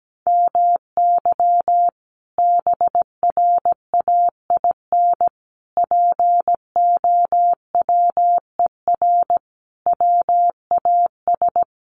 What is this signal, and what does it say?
Morse code, heard as a keyed tone, about 17 wpm, 700 Hz. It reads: MY BRAIN POWER WAS
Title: Sentences from Top 1400 Words 17wpm